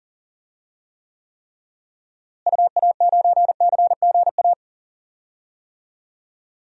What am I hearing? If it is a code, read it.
Morse code, heard as a keyed tone, about 38 wpm, 700 Hz. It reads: UA9CGA
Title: DXCC-100 Call Signs - 6 Characters - Repeated 3x - 40wpm